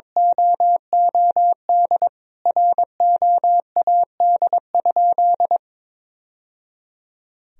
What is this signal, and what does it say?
Morse code, heard as a keyed tone, about 22 wpm, 700 Hz. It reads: OOD ROAD?
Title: Sentences from Top 1000 Words - Review Part 2 - 22wpm